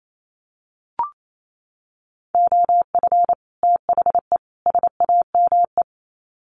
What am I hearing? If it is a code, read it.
Morse code, heard as a keyed tone, about 28 wpm, 700 Hz. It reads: OF THE SAME
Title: Sets of 3 Words 28wpm